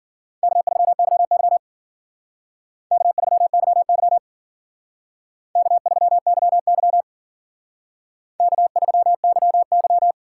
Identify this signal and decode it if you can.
Morse code, heard as a keyed tone, about 51 wpm, 700 Hz. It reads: K3YY K3YY K3YY K3YY